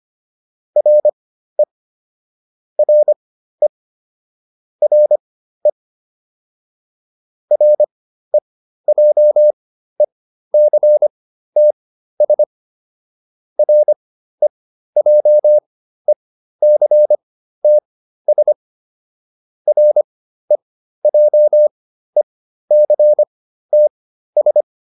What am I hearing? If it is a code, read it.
Morse code, heard as a keyed tone, about 25 wpm, spaced as for 13 wpm, 600 Hz. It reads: RE RE RE REJECTS REJECTS REJECTS